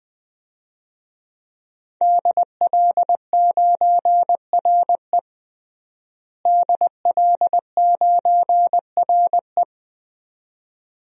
Morse code, heard as a keyed tone, about 20 wpm, 700 Hz. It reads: DL9RE DL9RE
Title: DXCC-100 Call Signs - 5 Characters - Repeated 2x - 20wpm